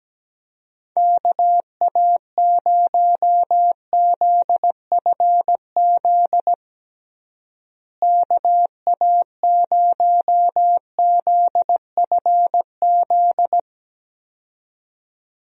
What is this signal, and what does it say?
Morse code, heard as a keyed tone, about 17 wpm, 700 Hz. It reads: KA0ZFZ KA0ZFZ
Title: USA Call Signs - 6 Characters - Repeated 2x - 17wpm